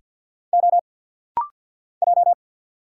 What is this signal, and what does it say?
Morse code, heard as a keyed tone, about 48 wpm, 700 Hz. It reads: O J
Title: Single Letter-Number - Rapid-Fire 50wpm